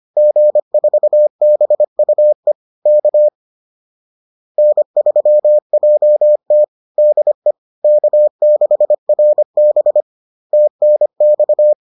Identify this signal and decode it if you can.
Morse code, heard as a keyed tone, about 25 wpm, 600 Hz. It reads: G4BUE K N3JT DE K6RB TNX